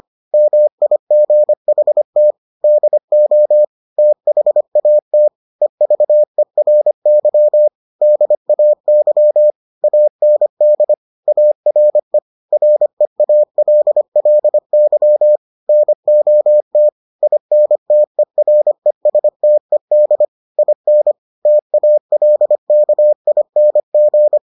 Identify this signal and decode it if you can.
Morse code, heard as a keyed tone, about 25 wpm, 600 Hz. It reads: MIGHT DO THAT EVERY DAY AND ARE REALLY NOT INTERESTED IN TALKING